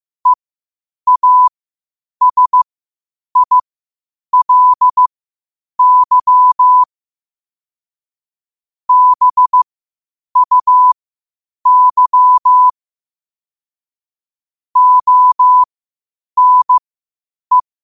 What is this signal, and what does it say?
Morse code, heard as a keyed tone, about 15 wpm, spaced as for 8 wpm, 1000 Hz. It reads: EASILY BUY ONE
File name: qotdfiles / 8wpm